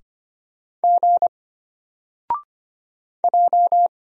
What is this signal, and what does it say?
Morse code, heard as a keyed tone, about 25 wpm, 700 Hz. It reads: G J
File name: Single-Letter-Number-25wpm